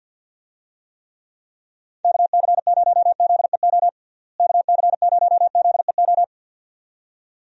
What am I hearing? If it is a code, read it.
Morse code, heard as a keyed tone, about 50 wpm, 700 Hz. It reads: KC0ZEO KC0ZEO
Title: USA Call Signs - 6 Characters - Repeated 2x - 50wpm